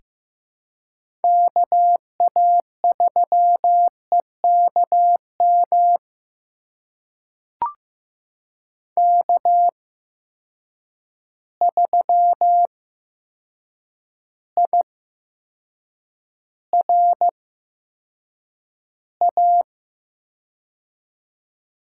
Morse code, heard as a keyed tone, about 15 wpm, spaced as for 7 wpm, 700 Hz. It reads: KA3EKM K3IRA
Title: USA Call Signs - 8x Character Spacing - 15wpm